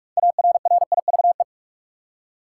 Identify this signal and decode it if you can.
Morse code, heard as a keyed tone, about 42 wpm, 700 Hz. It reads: ARRIVE
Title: Six Letter Words - 45wpm